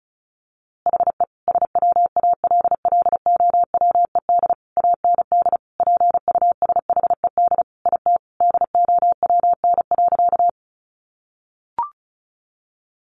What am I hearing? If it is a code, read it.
Morse code, heard as a keyed tone, about 34 wpm, 700 Hz. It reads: HE SWALLOWED AND PUSHED IT DOWN.
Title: A Sound Of Thunder 35wpm